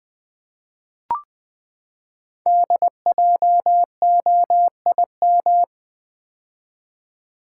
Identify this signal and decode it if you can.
Morse code, heard as a keyed tone, about 20 wpm, 700 Hz. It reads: DJOIM
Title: Code Groups - 5 Letters - 20wpm